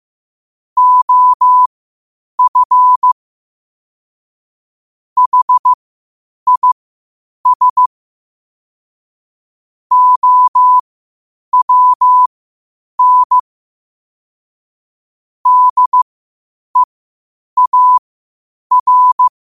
Morse code, heard as a keyed tone, about 15 wpm, spaced as for 8 wpm, 1000 Hz. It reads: OF HIS OWN DEAR
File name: 8wpm